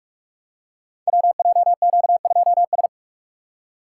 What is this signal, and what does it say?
Morse code, heard as a keyed tone, about 44 wpm, 700 Hz. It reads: WJQ2S
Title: Code Groups - 5 Letter:Numbers - 45wpm